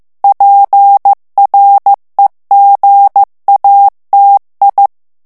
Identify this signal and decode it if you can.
Morse code, heard as a keyed tone, about 15 wpm, 800 Hz. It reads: PREGATI
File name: morse mp3 / ITALIEN texte clair 2 vitesse 14 mots minute